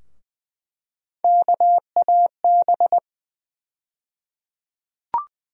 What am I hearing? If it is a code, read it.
Morse code, heard as a keyed tone, about 20 wpm, 700 Hz. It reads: KAB
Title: USA Call Sign Postfixes - 3 Letters 20wpm